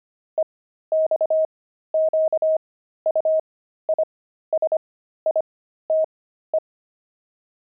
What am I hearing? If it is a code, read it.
Morse code, heard as a keyed tone, about 25 wpm, spaced as for 13 wpm, 630 Hz. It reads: EXQUISITE
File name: pangram-fs-13-cs-25-opal